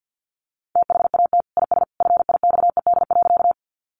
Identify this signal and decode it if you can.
Morse code, heard as a keyed tone, about 50 wpm, 700 Hz. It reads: THAT IS FIXED.
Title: Sentences from Top 1100 Words 50wpm